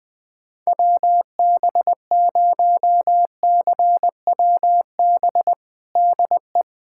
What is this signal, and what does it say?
Morse code, heard as a keyed tone, about 21 wpm, 700 Hz. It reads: WB0CWB DE
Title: USA Callsign DE Callsign 20wpm